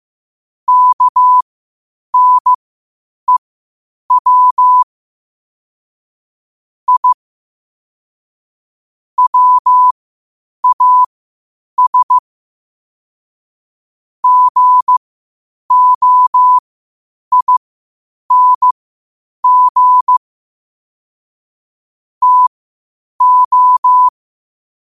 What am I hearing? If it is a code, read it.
Morse code, heard as a keyed tone, about 15 wpm, spaced as for 8 wpm, 1000 Hz. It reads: KNEW I WAS GOING TO
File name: qotdfiles / 8wpm